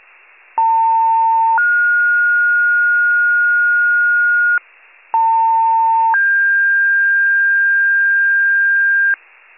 Two-tone page, heard as a one-second tone, then a three-second tone, about 900 then 1430 Hz, about 900 then 1600 Hz, with noise. It is heard from 0.6 to 9.2 s.
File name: pager / 73 RESCUE / wcalert